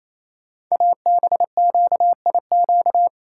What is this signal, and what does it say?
Morse code, heard as a keyed tone, about 28 wpm, 700 Hz. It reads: ABQIQ
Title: Code Groups - 5 Letters - 28wpm